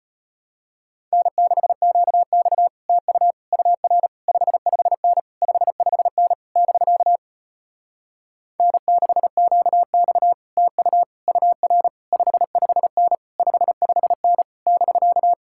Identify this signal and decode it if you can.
Morse code, heard as a keyed tone, about 33 wpm, 700 Hz. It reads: N6QX TU UR 55N 55N <BK> N6QX TU UR 55N 55N <BK>